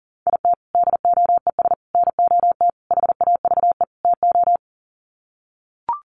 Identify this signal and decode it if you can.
Morse code, heard as a keyed tone, about 40 wpm, 700 Hz. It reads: IT DOES NOT HAVE TO
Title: Sets of 5 Words 40wpm